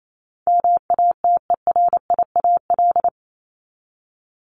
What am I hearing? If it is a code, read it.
Morse code, heard as a keyed tone, about 28 wpm, 700 Hz. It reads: MATERIAL
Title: ES:S Suffix - Top 100 Words with Base Words 28wpm